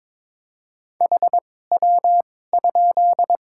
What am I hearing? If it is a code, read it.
Morse code, heard as a keyed tone, about 22 wpm, spaced as for 16 wpm, 700 Hz. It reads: HW?
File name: Top 90 QSO Elements - 2x Character Spacing - 22wpm